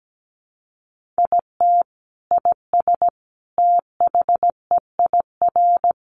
Morse code, heard as a keyed tone, about 17 wpm, 700 Hz. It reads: IT IS THEIR
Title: Sentences from Top 1000 Words 17wpm